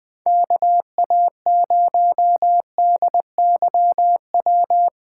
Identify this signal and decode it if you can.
Morse code, heard as a keyed tone, about 23 wpm, 700 Hz. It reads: KA0DYW
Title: Speed Racing - USA Call Signs 20wpm